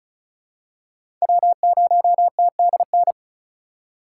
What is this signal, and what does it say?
Morse code, heard as a keyed tone, about 36 wpm, 700 Hz. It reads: W0TDN
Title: USA Call Signs - 5 Characters - 3x Character Spacing - 35wpm